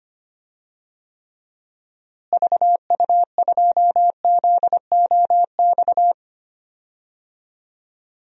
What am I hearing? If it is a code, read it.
Morse code, heard as a keyed tone, about 25 wpm, 700 Hz. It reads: VU2ZOX
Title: DXCC-100 Call Signs - 6 Characters - Repeated 2x - 25wpm